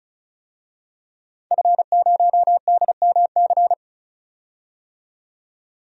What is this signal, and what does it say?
Morse code, heard as a keyed tone, about 36 wpm, 700 Hz. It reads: F0DMC
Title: DXCC-100 Call Signs - 5 Characters - 35wpm